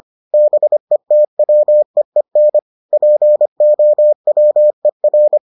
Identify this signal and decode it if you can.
Morse code, heard as a keyed tone, about 25 wpm, 600 Hz. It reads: BETWEEN POWER